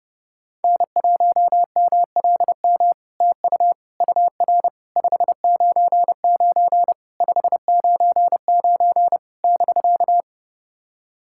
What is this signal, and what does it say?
Morse code, heard as a keyed tone, about 28 wpm, 700 Hz. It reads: N1MLM TU UR 599 599 <BK>
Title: SOTA QSO Practice 30wpm